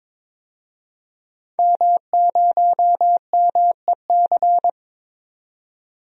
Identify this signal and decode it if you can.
Morse code, heard as a keyed tone, about 22 wpm, 700 Hz. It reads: M0MEC
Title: DXCC-100 Call Signs - 5 Characters - 22wpm